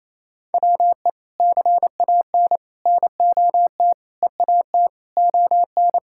Morse code, heard as a keyed tone, about 28 wpm, 700 Hz. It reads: WE CAN NOT EAT ON
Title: Sentences from Top 1000 Words - Review Part 6 - 28wpm